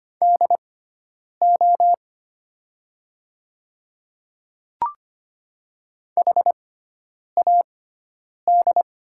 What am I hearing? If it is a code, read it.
Morse code, heard as a keyed tone, about 25 wpm, spaced as for 9 wpm, 700 Hz. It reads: DO HAD
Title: 6x Character Spacing - Top 200 Words - 25wpm